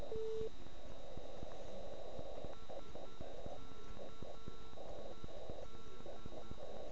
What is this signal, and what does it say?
Morse code, heard as a keyed tone, about 9 wpm, 1340 Hz. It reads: SKED